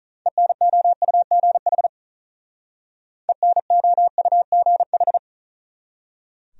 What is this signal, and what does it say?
Morse code, heard as a keyed tone, about 45 wpm, 700 Hz. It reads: ENOUGH ENOUGH